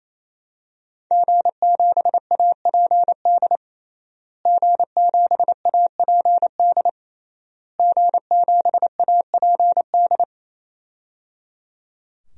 Morse code, heard as a keyed tone, about 28 wpm, 700 Hz. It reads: G7APD G7APD G7APD
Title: DXCC-100 Call Signs - 5 Characters - Repeated 3x - 28wpm